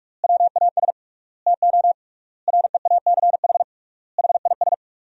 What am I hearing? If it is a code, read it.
Morse code, heard as a keyed tone, about 47 wpm, 700 Hz. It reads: WAS TO REACH HIS
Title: 3x Word-Spacing - Sentences from Top 400 Words - 45wpm